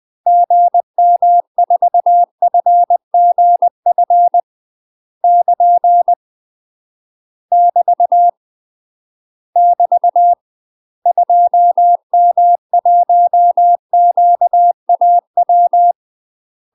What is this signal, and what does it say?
Morse code, heard as a keyed tone, about 20 wpm, 700 Hz. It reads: GM4FGF <KN> <BT> <BT> 2M1QAW
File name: UK MASTER QSOs TRAINING / 20WPM / UK MASTER QSOs 0010